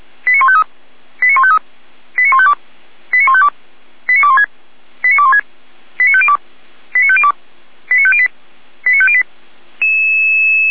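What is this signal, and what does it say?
Five-tone selective call (ZVEI), heard as quick runs of five tones, tones of about 70 ms. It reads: 89153 89153 89152 89152 89216 89216 89502 89502 89508 89508
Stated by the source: ZVEI five-tone selective call received on an SDR.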